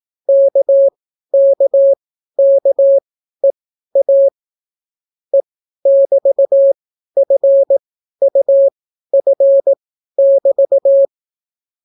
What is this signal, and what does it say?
Morse code, heard as a keyed tone, about 18 wpm, spaced as for 12 wpm, 550 Hz. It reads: KKKEA E<BT>FUF<BT>